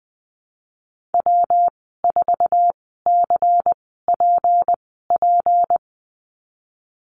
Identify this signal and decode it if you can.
Morse code, heard as a keyed tone, about 20 wpm, spaced as for 15 wpm, 700 Hz. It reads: W4CPP
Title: 161 Lesson 26g - Letter g - Call Signs 20wpm